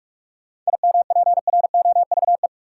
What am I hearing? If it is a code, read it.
Morse code, heard as a keyed tone, about 42 wpm, 700 Hz. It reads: IMPROVE
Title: Sentences from Top 800 Words - New Word Drill - 45wpm